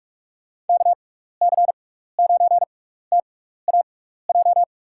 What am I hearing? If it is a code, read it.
Morse code, heard as a keyed tone, about 41 wpm, spaced as for 15 wpm, 700 Hz. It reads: KC9TAJ
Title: USA Call Signs - 6 Characters - 6x Character Spacing - 45wpm